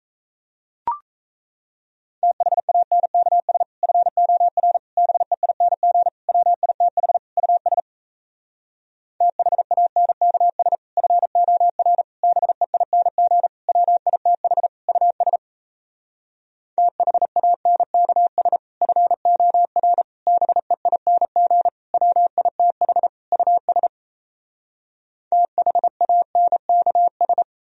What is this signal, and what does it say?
Morse code, heard as a keyed tone, about 38 wpm, 700 Hz. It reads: THANKS FOR BEING WITH US THANKS FOR BEING WITH US THANKS FOR BEING WITH US THANKS